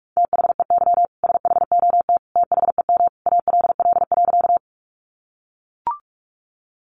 Morse code, heard as a keyed tone, about 44 wpm, 700 Hz. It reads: THEY SHOT THEM ALL.
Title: Sentences from Top 1100 Words 45wpm